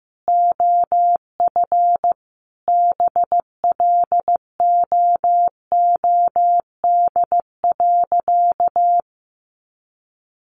Morse code, heard as a keyed tone, about 15 wpm, 700 Hz. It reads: OF BLOOD.
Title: Sentences from Top 1100 Words 15wpm